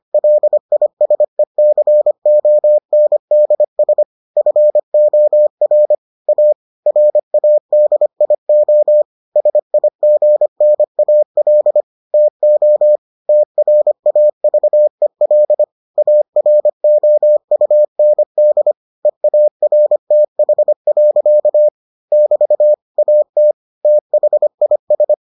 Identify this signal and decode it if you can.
Morse code, heard as a keyed tone, about 25 wpm, 600 Hz. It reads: LISECONDS FOR A RADIO SIGNAL TO TRAVEL AROUND EARTH. <BT> AT THIS